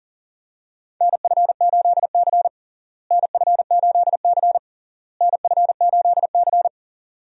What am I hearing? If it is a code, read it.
Morse code, heard as a keyed tone, about 37 wpm, 700 Hz. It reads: NF8C NF8C NF8C
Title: USA Call Signs - 4 Characters - Repeated 3x - 40wpm